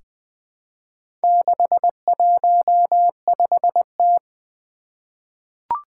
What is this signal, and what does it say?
Morse code, heard as a keyed tone, about 20 wpm, 700 Hz. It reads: 615T
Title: Four Challenging Letter-Number Combinations 20wpm